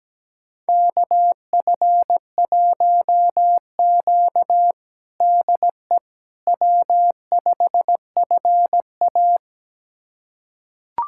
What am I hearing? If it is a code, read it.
Morse code, heard as a keyed tone, about 17 wpm, 700 Hz. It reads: KF1Q DE W5FA
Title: USA Callsign DE Callsign - 4 Characters - 17wpm